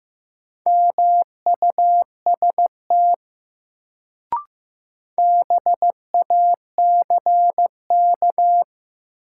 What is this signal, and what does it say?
Morse code, heard as a keyed tone, about 15 wpm, 700 Hz. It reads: MUST BACK
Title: Sets of 2 Words - New Word Drill - 15wpm